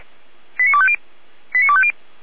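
Five-tone selective call (ZVEI), heard as quick runs of five tones, tones of about 70 ms. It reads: 89250 89250
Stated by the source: ZVEI five-tone selective call received with an SDR, on the audio server of Feuerwehr Orlamünde (fire brigade).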